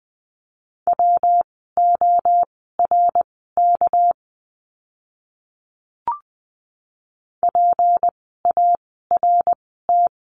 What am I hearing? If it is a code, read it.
Morse code, heard as a keyed tone, about 20 wpm, spaced as for 15 wpm, 700 Hz. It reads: WORK PART
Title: 244 Lesson 39f - Letter z - Words 20wpm